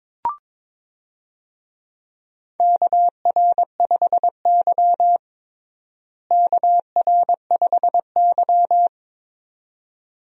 Morse code, heard as a keyed tone, about 22 wpm, 700 Hz. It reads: KR5Y KR5Y
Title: USA Call Signs - 4 Characters - Repeated 2x - 22wpm